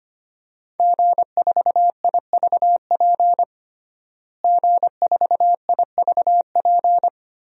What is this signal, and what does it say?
Morse code, heard as a keyed tone, about 25 wpm, 700 Hz. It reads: G4IVP G4IVP
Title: DXCC-100 Call Signs - 5 Characters - Repeated 3x - 25wpm